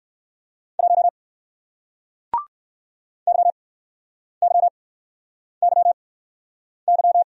Morse code, heard as a keyed tone, about 61 wpm, 700 Hz. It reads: 1 Y Y Y Y